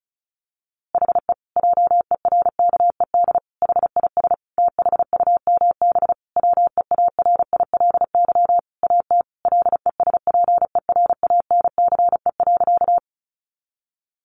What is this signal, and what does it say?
Morse code, heard as a keyed tone, about 34 wpm, 700 Hz. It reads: HE JERKED HIS THUMB WEARILY AT LESPERANCE.
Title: A Sound Of Thunder 35wpm